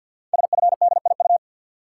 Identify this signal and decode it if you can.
Morse code, heard as a keyed tone, about 53 wpm, 700 Hz. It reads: SFDIU
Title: Code Groups - 5 Letters - 50wpm